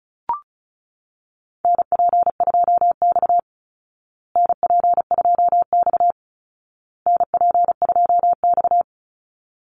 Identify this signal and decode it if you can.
Morse code, heard as a keyed tone, about 34 wpm, 700 Hz. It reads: NP2X NP2X NP2X
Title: Call Signs Repeated 2x - 35wpm